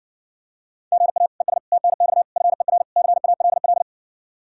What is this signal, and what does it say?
Morse code, heard as a keyed tone, about 67 wpm, 700 Hz. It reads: GA ES TNX FER CALL